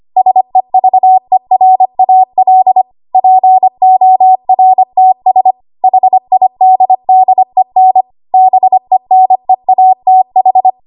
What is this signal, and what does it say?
Morse code, heard as a keyed tone, about 25 wpm, 750 Hz. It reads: SEVERAL PORTS HIDDEN BENEATH
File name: Archive / 25 WPM / 25WPM